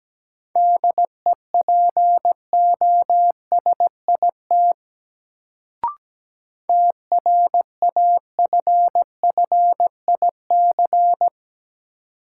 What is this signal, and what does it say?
Morse code, heard as a keyed tone, about 17 wpm, 700 Hz. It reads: DEPOSIT TRAFFIC
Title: Sentences from Top 1300 Words - New Word Drill Alt - 17wpm